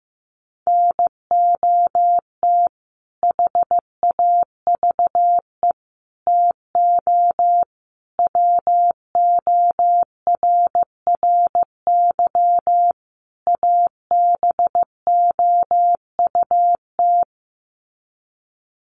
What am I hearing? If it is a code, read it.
Morse code, heard as a keyed tone, about 15 wpm, 700 Hz. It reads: NOT HAVE TO WORRY ABOUT
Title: Sets of 5 Words 15wpm